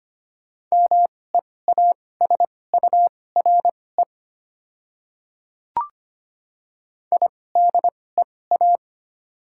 Morse code, heard as a keyed tone, about 25 wpm, spaced as for 18 wpm, 700 Hz. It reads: MEASURE IDEA
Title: 2x Character Spacing - Top 500 Words - 25wpm